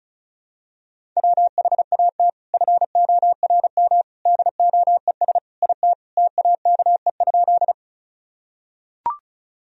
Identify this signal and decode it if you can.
Morse code, heard as a keyed tone, about 34 wpm, 700 Hz. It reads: WHAT FORM DOES IT TAKE?
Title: Sentences from Top 1000 Words - Review Part 1 - 35wpm